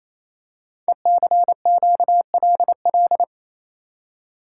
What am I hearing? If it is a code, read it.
Morse code, heard as a keyed tone, about 27 wpm, 700 Hz. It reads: ECQLL
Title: Code Groups - 5 Letters - 28wpm